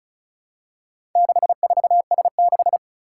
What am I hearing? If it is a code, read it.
Morse code, heard as a keyed tone, about 36 wpm, 690 Hz. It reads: 64S6
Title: Four Challenging Letter-Number Combinations 35wpm